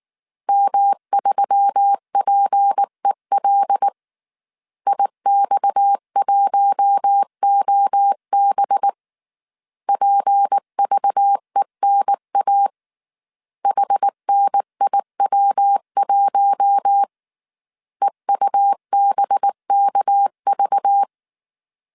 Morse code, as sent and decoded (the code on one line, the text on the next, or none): -- ...-- .--. .-.. / .. -..- .---- --- -... / .--. ...- . -. .- / .... -. .. .-- .---- / . ..- -... -.- ...-
M3PEL IX1OB PVENA HNIW1 EUBKV